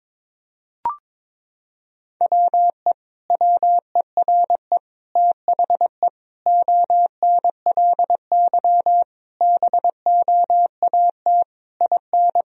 Morse code, heard as a keyed tone, about 22 wpm, 700 Hz. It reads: WE WERE THE ONLY BOAT IN
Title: Sentences from Top 1000 Words - Review Part 1 - 22wpm